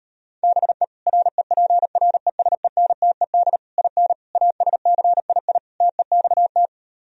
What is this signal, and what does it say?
Morse code, heard as a keyed tone, about 38 wpm, 700 Hz. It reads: BE REPRESENTED IN ASCII TEXT